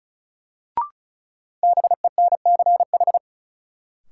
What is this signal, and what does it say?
Morse code, heard as a keyed tone, about 36 wpm, 700 Hz. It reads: BENCH
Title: Sentences from Top 1400 Words - New Word Drill Alt - 35wpm